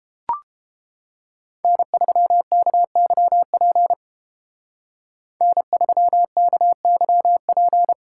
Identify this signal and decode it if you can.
Morse code, heard as a keyed tone, about 32 wpm, 700 Hz. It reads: N3KYP N3KYP